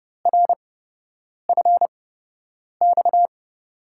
Morse code, heard as a keyed tone, about 31 wpm, spaced as for 8 wpm, 700 Hz. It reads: RFX
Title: USA Call Signs - 8x Character Spacing - 30wpm